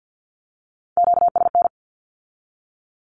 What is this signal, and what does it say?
Morse code, heard as a keyed tone, about 50 wpm, 700 Hz. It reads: QSN
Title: Q-Codes with Spoken Meaning - 50wpm